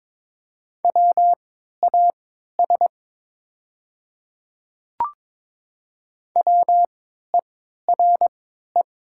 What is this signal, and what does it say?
Morse code, heard as a keyed tone, about 23 wpm, spaced as for 13 wpm, 700 Hz. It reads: WAS WERE
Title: 3x Character Spacing - Top 100 Words - 22wpm